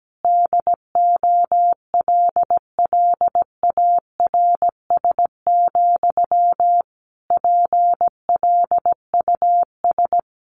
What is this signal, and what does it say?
Morse code, heard as a keyed tone, about 17 wpm, 700 Hz. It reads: DOLLARS, PLUS
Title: A Sound Of Thunder 17wpm